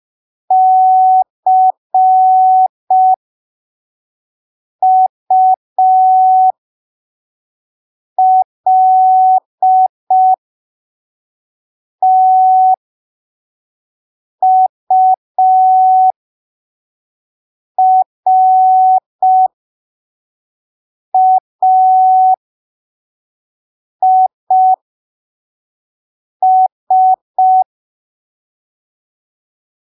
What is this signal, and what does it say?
Morse code, heard as a keyed tone, about 5 wpm, spaced as for 3 wpm, 750 Hz. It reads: CULTURAIS
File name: arquivos / texto / texto 30 05ppm